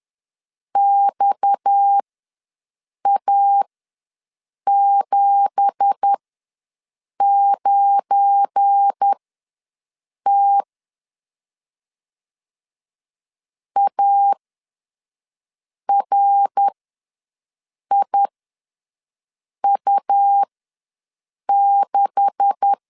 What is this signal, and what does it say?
Morse code, heard as a keyed tone, about 10 wpm, spaced as for 5 wpm, 800 Hz. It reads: XA79T ARIU6
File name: LEZIONI / 9 2867 ws 10wpm 4215